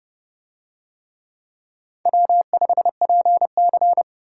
Morse code, heard as a keyed tone, about 31 wpm, 700 Hz. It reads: W5PC